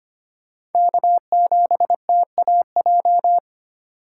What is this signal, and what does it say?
Morse code, heard as a keyed tone, about 25 wpm, 700 Hz. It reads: K7TAJ
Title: USA Call Signs - 5 Characters - 2x Character Spacing - 25wpm